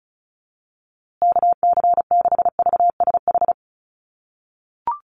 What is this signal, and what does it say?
Morse code, heard as a keyed tone, about 34 wpm, 700 Hz. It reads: KC6VSH
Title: USA Call Signs Repeated 2x 35wpm